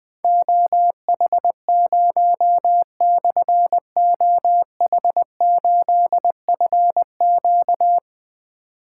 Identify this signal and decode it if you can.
Morse code, heard as a keyed tone, about 20 wpm, 700 Hz. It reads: OH0/OH8FQ
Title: Difficult Call Signs 20wpm